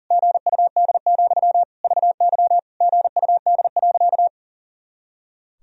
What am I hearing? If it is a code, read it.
Morse code, heard as a keyed tone, about 40 wpm, 700 Hz. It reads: GUD, VY GUD.
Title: Sentences from Top 200 Words and Abbreviations 40wpm